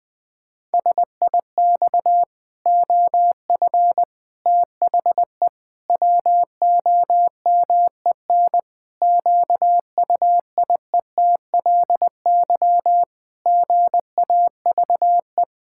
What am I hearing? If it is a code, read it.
Morse code, heard as a keyed tone, about 20 wpm, 700 Hz. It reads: SIX OF THE WOMEN QUIETLY GAVE
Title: Pangram Challenge v2 - 20wpm